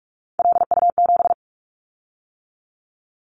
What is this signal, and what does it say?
Morse code, heard as a keyed tone, about 46 wpm, 690 Hz. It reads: LU7
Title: Callsign Prefixes for Top 50 DXCC - 45wpm